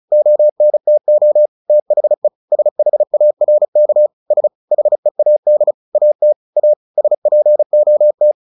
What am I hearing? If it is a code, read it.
Morse code, heard as a keyed tone, about 34 wpm, 600 Hz. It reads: ONTO THE SHARK S HEAD AT A SPOT
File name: the old man and the sea 2 35wpm